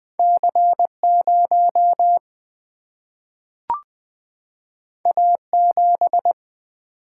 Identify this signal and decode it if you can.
Morse code, heard as a keyed tone, about 20 wpm, 700 Hz. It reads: C0 A7
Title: Single-Letter Single-Number 20wpm